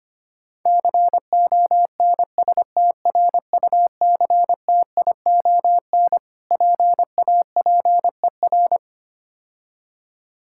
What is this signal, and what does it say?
Morse code, heard as a keyed tone, about 25 wpm, 700 Hz. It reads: CONSTRUCTION PAPER
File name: practice / 25wpm / Difficult Words - Speed-Racing with Hints - 25wpm